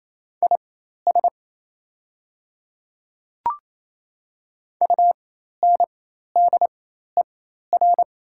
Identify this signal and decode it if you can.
Morse code, heard as a keyed tone, about 28 wpm, spaced as for 13 wpm, 700 Hz. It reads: IS UNDER
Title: 4x Character Spacing - Top 200 Words - 28wpm